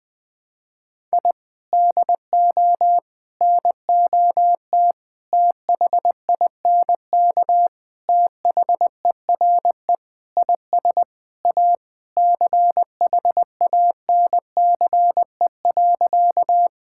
Morse code, heard as a keyed tone, about 20 wpm, 700 Hz. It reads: I DO NOT THINK THERE IS A CHANCE.
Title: Sentences with 5-Word Phrases - 20wpm